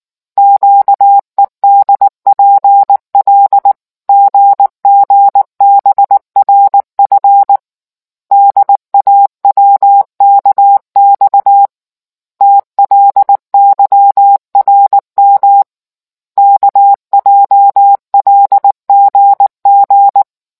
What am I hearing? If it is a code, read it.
Morse code, heard as a keyed tone, about 19 wpm, 800 Hz. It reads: QEDPL GGBRF DAWKX TLYRM KJLGG